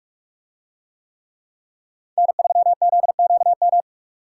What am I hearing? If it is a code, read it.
Morse code, heard as a keyed tone, about 42 wpm, 700 Hz. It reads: N3ZQM